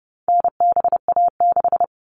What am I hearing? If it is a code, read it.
Morse code, heard as a keyed tone, about 30 wpm, 700 Hz. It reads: NBA6
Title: All Cut Numbers - 4-Digit 30wpm